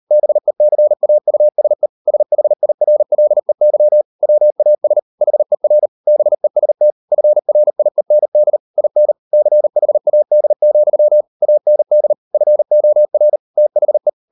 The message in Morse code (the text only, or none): BECAUSE SHIRLEY WAS HER BEST FRIEND IN CHAD, AND FOR THE